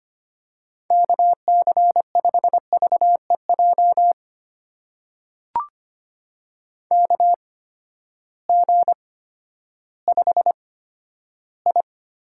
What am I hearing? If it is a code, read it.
Morse code, heard as a keyed tone, about 25 wpm, spaced as for 12 wpm, 700 Hz. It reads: KC5VEJ KG5I
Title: USA Call Signs - 6 Characters - 8x Character Spacing - 25wpm